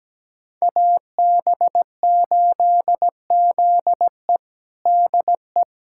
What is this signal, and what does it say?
Morse code, heard as a keyed tone, about 17 wpm, 700 Hz. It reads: AB8ZE DE
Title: USA Callsign DE Callsign - 5 Characters - 17wpm